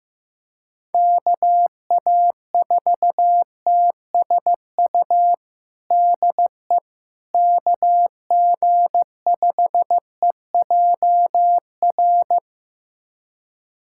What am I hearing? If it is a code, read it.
Morse code, heard as a keyed tone, about 15 wpm, 700 Hz. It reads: KA4TSU DE KG5EJR
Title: USA Callsign DE Callsign - 6 Characters - 15wpm